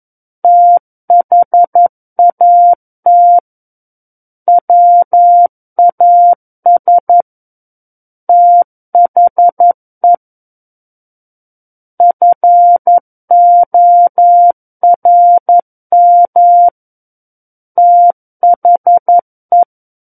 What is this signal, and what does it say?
Morse code, heard as a keyed tone, about 11 wpm, 700 Hz. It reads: THAT WAS THE FORM THE
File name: 11 WPM